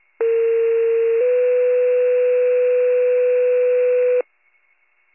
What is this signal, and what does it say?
Two-tone page, heard as a one-second tone, then a three-second tone, about 460 then 500 Hz. It is heard from 0.2 to 4.2 s.